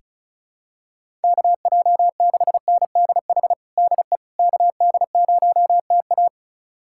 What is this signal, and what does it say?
Morse code, heard as a keyed tone, about 34 wpm, 700 Hz. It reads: KJ6NDH DE KD0TA
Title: USA Callsign DE Callsign 35wpm